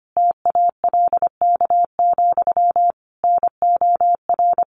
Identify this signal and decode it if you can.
Morse code, heard as a keyed tone, about 25 wpm, 700 Hz. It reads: TALK, NOR
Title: Sentences from Top 1500 Words 25wpm